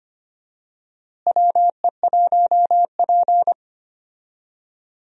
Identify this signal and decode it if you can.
Morse code, heard as a keyed tone, about 25 wpm, 700 Hz. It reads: WE1P
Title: USA Call Signs - 4 Characters - Repeated 3x - 25wpm